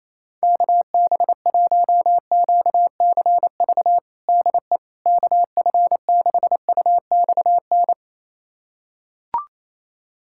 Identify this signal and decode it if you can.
Morse code, heard as a keyed tone, about 28 wpm, 700 Hz. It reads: KB1QCV DE KF6UXN
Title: USA Callsign DE Callsign - 6 Characters - 28wpm